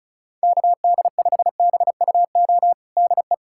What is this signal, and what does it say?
Morse code, heard as a keyed tone, about 36 wpm, 700 Hz. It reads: KD5BUO DE
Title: USA Callsign DE Callsign - 6 Characters - 35wpm